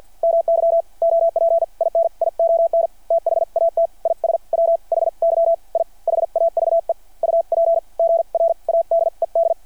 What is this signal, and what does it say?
Morse code, heard as a keyed tone, about 52 wpm, spaced as for 42 wpm, 600 Hz. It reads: MY OPINION THAT IS WHY I HAVE UPGRADED
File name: archived / CRSnet